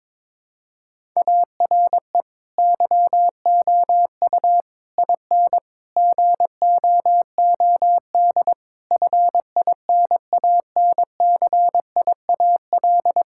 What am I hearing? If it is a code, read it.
Morse code, heard as a keyed tone, about 21 wpm, 700 Hz. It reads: ARE YOU IN GOOD FINANCIAL
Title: Sentences from Top 1000 Words - Review Part 5 - 22wpm